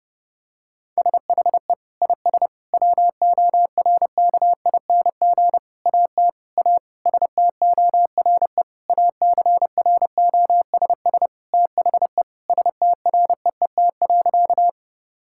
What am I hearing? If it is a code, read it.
Morse code, heard as a keyed tone, about 32 wpm, 700 Hz. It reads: SHE IS WORKING AT A STORE ACROSS THE STREET.
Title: Sentences from Top 1000 Words - Review Part 5 - 30wpm